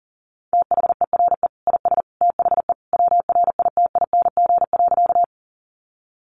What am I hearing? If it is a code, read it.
Morse code, heard as a keyed tone, about 40 wpm, 700 Hz. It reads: THERE IS THE WRITING.
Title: Sentences from Top 800 Words 40wpm